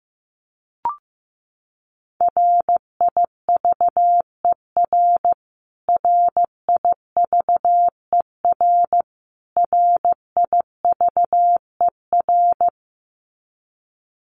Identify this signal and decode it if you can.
Morse code, heard as a keyed tone, about 15 wpm, 700 Hz. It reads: RIVER RIVER RIVER
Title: Top 300 Words Repeated 15wpm